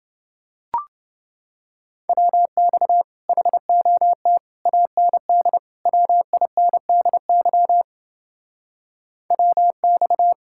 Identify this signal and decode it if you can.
Morse code, heard as a keyed tone, about 29 wpm, 700 Hz. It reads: WX HOT AND WINDY WX